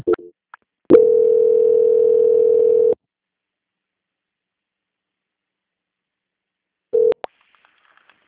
Ringback tone (North America), 440 + 480 Hz.
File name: out- -2002- - - .12